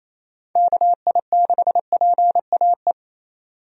Morse code, heard as a keyed tone, about 28 wpm, 700 Hz. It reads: KI6PAE
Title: USA Call Signs - 6 Characters - 28wpm